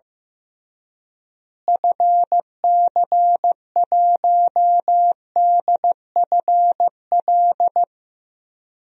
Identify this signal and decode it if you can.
Morse code, heard as a keyed tone, about 15 wpm, 700 Hz. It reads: FC1DFL
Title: DXCC-100 Call Signs - 6 Characters - Repeated 2x - 15wpm